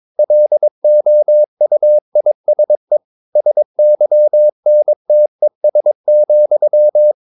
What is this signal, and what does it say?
Morse code, heard as a keyed tone, about 22 wpm, 600 Hz. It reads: LOUISE SYNTES,
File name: Ferie QRS 22wpm